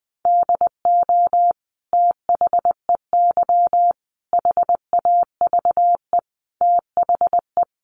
Morse code, heard as a keyed tone, about 20 wpm, 700 Hz. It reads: DO THEY HAVE THE
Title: Sentences from Top 1300 Words 20wpm